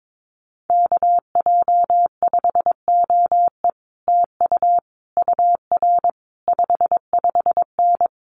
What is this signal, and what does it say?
Morse code, heard as a keyed tone, about 22 wpm, 700 Hz. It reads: KJ5OE TU UR 55N
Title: Common QSO Phrases Encore 22wpm